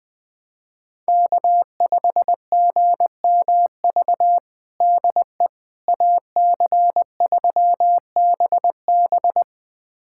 Morse code, heard as a keyed tone, about 20 wpm, 700 Hz. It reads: K5GMV DE AC3BB
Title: USA Callsign DE Callsign - 5 Characters - 20wpm